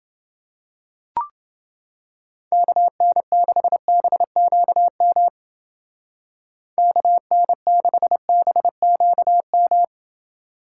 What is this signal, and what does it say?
Morse code, heard as a keyed tone, about 28 wpm, 700 Hz. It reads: KN6BQM KN6BQM